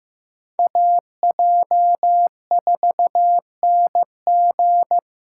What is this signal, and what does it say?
Morse code, heard as a keyed tone, about 18 wpm, 700 Hz. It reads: AJ4NG
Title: Speed Racing - USA 2x2 Call Signs 15wpm